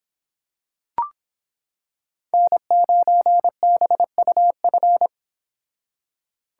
Morse code, heard as a keyed tone, about 26 wpm, 700 Hz. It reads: N9BUF